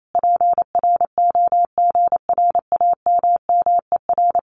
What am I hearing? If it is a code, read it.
Morse code, heard as a keyed tone, about 28 wpm, 700 Hz. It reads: PROGRAMMER
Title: Common QSO Phrases Encore 28wpm